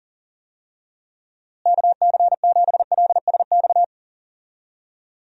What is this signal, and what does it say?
Morse code, heard as a keyed tone, about 40 wpm, 700 Hz. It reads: KC7LSX
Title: USA Call Signs - 6 Characters - Repeated 3x - 40wpm